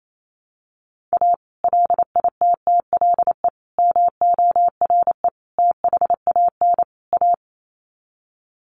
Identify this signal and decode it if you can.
Morse code, heard as a keyed tone, about 28 wpm, 700 Hz. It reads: A LITTLE MORE THAN A
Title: Sets of 5 Words Encore2 28wpm